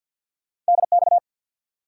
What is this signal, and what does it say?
Morse code, heard as a keyed tone, about 48 wpm, 700 Hz. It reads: DX
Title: Top 180 QSO Elements 50wpm